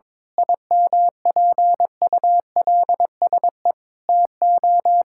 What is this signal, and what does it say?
Morse code, heard as a keyed tone, about 22 wpm, 700 Hz. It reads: IMPULSE TO
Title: Pangram Challenge v2 - 22wpm